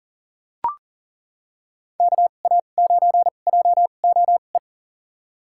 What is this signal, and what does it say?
Morse code, heard as a keyed tone, about 37 wpm, spaced as for 28 wpm, 700 Hz. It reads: KA9JOE
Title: USA Call Signs - 6 Characters - 2x Character Spacing - 40wpm